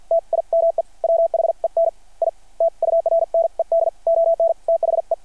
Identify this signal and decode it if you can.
Morse code, heard as a keyed tone, about 53 wpm, spaced as for 45 wpm, 660 Hz. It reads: TIME WHEN I TURNED ON THE
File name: archived / CRSnet